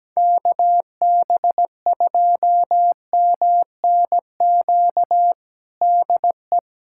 Morse code, heard as a keyed tone, about 17 wpm, 700 Hz. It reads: KB2MNQ DE
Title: USA Callsign DE Callsign - 6 Characters - 17wpm